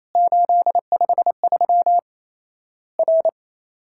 Morse code, heard as a keyed tone, about 28 wpm, 700 Hz. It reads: 853 R
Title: Worked All Europe DX Contest - QTC Exchange - 28wpm